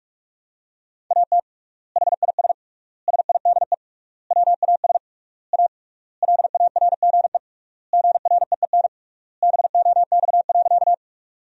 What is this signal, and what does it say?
Morse code, heard as a keyed tone, about 41 wpm, 700 Hz. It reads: AT HIS SIDE WAS A LARGE GREEN BOX.
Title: 3x Word-Spacing - Sentences from Top 500 Words - 45wpm